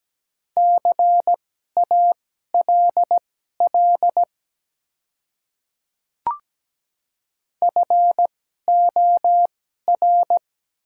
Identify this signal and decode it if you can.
Morse code, heard as a keyed tone, about 17 wpm, spaced as for 12 wpm, 700 Hz. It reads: CALL FOR
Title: 2x Character Spacing - Top 300 Words - 17wpm